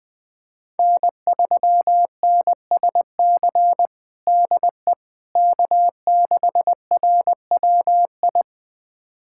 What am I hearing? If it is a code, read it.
Morse code, heard as a keyed tone, about 21 wpm, 700 Hz. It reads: N3NSC DE K6RWI
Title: USA Callsign DE Callsign - 5 Characters - 20wpm